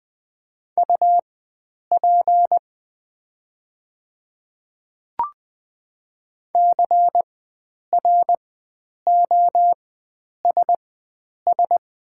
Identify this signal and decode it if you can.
Morse code, heard as a keyed tone, about 20 wpm, spaced as for 9 wpm, 700 Hz. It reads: UP CROSS
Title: 4x Character Spacing - Top 500 Words - 20wpm